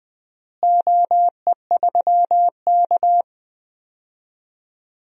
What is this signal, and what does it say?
Morse code, heard as a keyed tone, about 20 wpm, 700 Hz. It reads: OE3K
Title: DXCC-100 Call Signs - 4 Characters - Repeated 3x - 20wpm